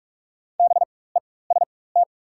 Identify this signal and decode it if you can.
Morse code, heard as a keyed tone, about 44 wpm, spaced as for 21 wpm, 690 Hz. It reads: BEST
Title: 4x Character Spacing - Top 500 Words - 45wpm